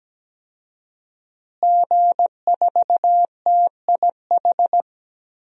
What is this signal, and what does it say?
Morse code, heard as a keyed tone, about 17 wpm, 700 Hz. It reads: G4TIH
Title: DXCC-100 Call Signs - 5 Characters - Repeated 2x - 17wpm